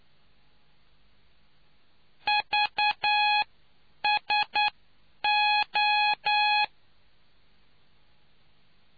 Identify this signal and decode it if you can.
Morse code, heard as a keyed tone, about 9 wpm, spaced as for 8 wpm, 840 Hz. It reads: VSO